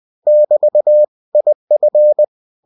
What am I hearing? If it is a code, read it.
Morse code, heard as a keyed tone, about 20 wpm, 600 Hz. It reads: <BT> IF
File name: cw-books / winnie-the-pooh / winnie-20wpm-page-0003